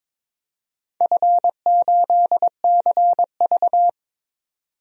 Morse code, heard as a keyed tone, about 22 wpm, 700 Hz. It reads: F8CV
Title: DXCC-100 Call Signs - 4 Characters - 22wpm